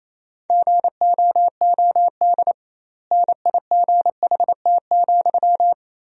Morse code, heard as a keyed tone, about 28 wpm, 700 Hz. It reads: GOOD NIGHT,
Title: 2x Word-Spacing - Sentences from Top 300 Words - 28wpm